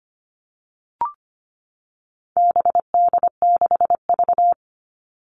Morse code, heard as a keyed tone, about 25 wpm, 700 Hz. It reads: BD6V
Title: All Cut Numbers - 4-Digit 25wpm